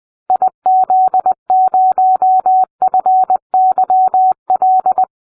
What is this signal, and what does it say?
Morse code, heard as a keyed tone, about 20 wpm, 750 Hz. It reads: IZ0FYL